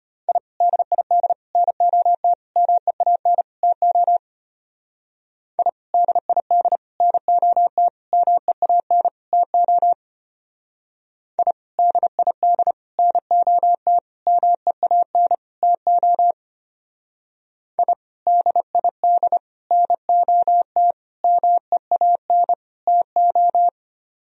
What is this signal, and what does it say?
Morse code, heard as a keyed tone, about 30 wpm, 700 Hz. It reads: I DID NOT MEAN TO I DID NOT MEAN TO I DID NOT MEAN TO I DID NOT MEAN TO
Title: Speed-Racing - Sets of 5 Words 25wpm